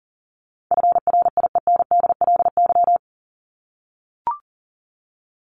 Morse code, heard as a keyed tone, about 40 wpm, 700 Hz. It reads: FRIENDLY
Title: LY Suffix - Top 100 Words with Base Words 40wpm